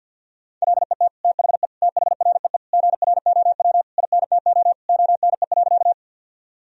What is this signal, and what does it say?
Morse code, heard as a keyed tone, about 53 wpm, 700 Hz. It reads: LET THE THREE GROW INTO ONE.
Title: Sentences from Top 1000 Words - Review Part 1 - 50wpm